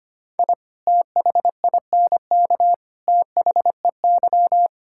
Morse code, heard as a keyed tone, about 25 wpm, 700 Hz. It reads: I THINK THEY
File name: practice / 25wpm / Sentences with 3-Word Phrases - 25wpm